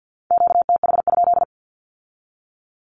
Morse code, heard as a keyed tone, about 50 wpm, 700 Hz. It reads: QTH?
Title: Q-Codes with Spoken Meaning - 50wpm